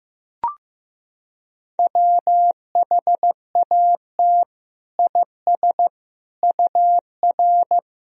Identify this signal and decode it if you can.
Morse code, heard as a keyed tone, about 15 wpm, 700 Hz. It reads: WHAT IS UR
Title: Sentences from Top 200 Words and Abbreviations 15wpm